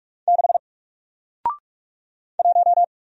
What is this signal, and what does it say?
Morse code, heard as a keyed tone, about 44 wpm, 700 Hz. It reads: 6 1
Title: Single Character (Letter:Number:Punctuation:Prosign) Alt 45wpm